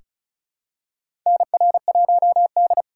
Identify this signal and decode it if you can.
Morse code, heard as a keyed tone, about 36 wpm, 700 Hz. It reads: NR1D
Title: USA Call Signs - 4 Characters - 3x Character Spacing - 35wpm